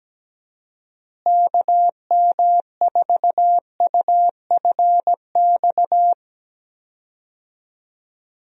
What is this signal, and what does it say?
Morse code, heard as a keyed tone, about 17 wpm, 700 Hz. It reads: KM4UFX
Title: USA Call Signs - 6 Characters - Repeated 2x - 17wpm